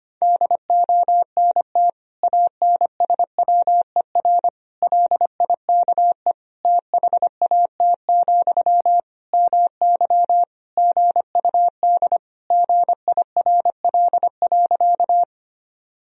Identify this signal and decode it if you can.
Morse code, heard as a keyed tone, about 25 wpm, 700 Hz. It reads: DONT ANSWER LIKE THAT, MY GUD GIRL.
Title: Sentences from Top 400 Words and Abbreviations 25wpm